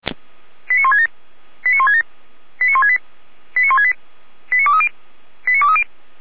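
Five-tone selective call (ZVEI), heard as quick runs of five tones, tones of about 70 ms. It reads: 89167 89167 89168 89168 89230 89230
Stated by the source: ZVEI five-tone alert received by SDR.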